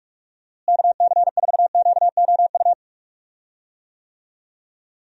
Morse code, heard as a keyed tone, about 41 wpm, 700 Hz. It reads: KC4QOU
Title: USA Call Signs - 6 Characters - Repeated 2x - 45wpm